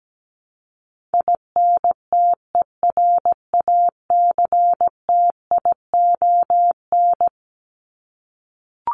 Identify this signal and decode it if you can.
Morse code, heard as a keyed tone, about 17 wpm, 700 Hz. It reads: INTERACTION
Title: Top 1251-1500 Words 17wpm